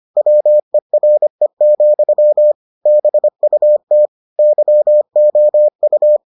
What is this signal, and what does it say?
Morse code, heard as a keyed tone, about 25 wpm, 600 Hz. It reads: WERE, BUT YOU